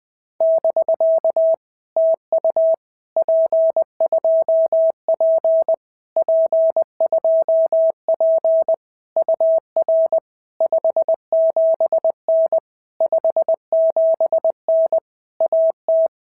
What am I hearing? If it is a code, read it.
Morse code, heard as a keyed tone, about 20 wpm, 650 Hz. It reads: <BK> TU P2P P2P UR 57N 57N AT
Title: POTA QSO Practice - 20wpm